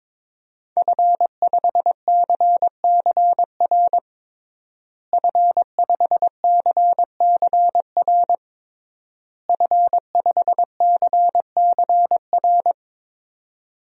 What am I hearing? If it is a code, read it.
Morse code, heard as a keyed tone, about 22 wpm, 700 Hz. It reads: F5CCR F5CCR F5CCR
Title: DXCC-100 Call Signs - 5 Characters - Repeated 3x - 22wpm